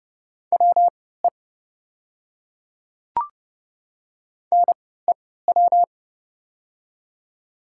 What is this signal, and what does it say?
Morse code, heard as a keyed tone, about 30 wpm, spaced as for 17 wpm, 700 Hz. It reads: WE NEW
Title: 3x Character Spacing - Top 200 Words - 30wpm